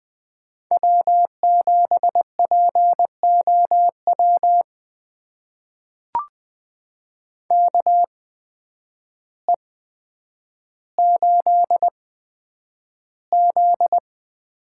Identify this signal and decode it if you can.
Morse code, heard as a keyed tone, about 20 wpm, spaced as for 9 wpm, 700 Hz. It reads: W7POW KE8Z
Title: USA Call Signs - 8x Character Spacing - 20wpm